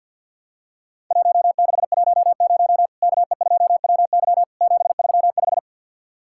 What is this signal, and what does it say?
Morse code, heard as a keyed tone, about 48 wpm, 700 Hz. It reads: 1610 KE2WY 735
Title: Worked All Europe DX Contest - QTC Exchange with Repeats - 50wpm